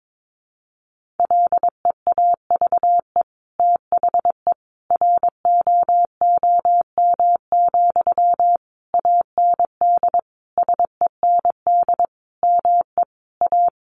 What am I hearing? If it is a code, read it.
Morse code, heard as a keyed tone, about 22 wpm, 700 Hz. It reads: LEAVE THE ROOM, AND SEND ME A
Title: Sentences from Top 1500 Words 22wpm